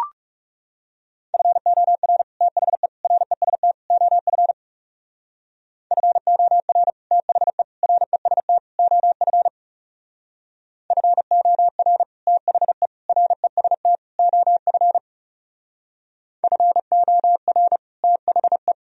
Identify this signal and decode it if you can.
Morse code, heard as a keyed tone, about 36 wpm, 700 Hz. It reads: FOR THE REST OF FOR THE REST OF FOR THE REST OF FOR THE
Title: Speed-Racing - Sets of 4 Words 30wpm